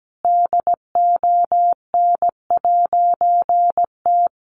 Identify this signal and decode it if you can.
Morse code, heard as a keyed tone, about 17 wpm, 700 Hz. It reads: DON'T
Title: A Sound Of Thunder 17wpm